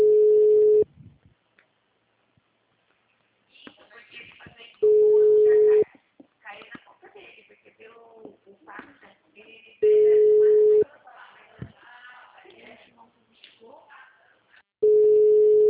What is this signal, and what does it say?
Ringback tone (Europe, 425 Hz), with speech.
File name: out- -2007- - - .35472